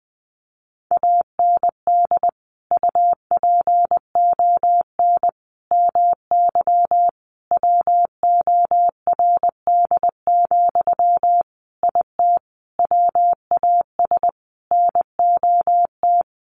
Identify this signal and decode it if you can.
Morse code, heard as a keyed tone, about 20 wpm, 700 Hz. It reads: AND UPON MY WORD, IT WAS NOT
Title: Sentences from Top 1100 Words 20wpm